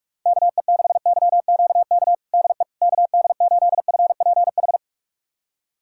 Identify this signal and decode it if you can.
Morse code, heard as a keyed tone, about 46 wpm, 700 Hz. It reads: KE6YQK DE KD8FPH
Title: USA Callsign DE Callsign 45wpm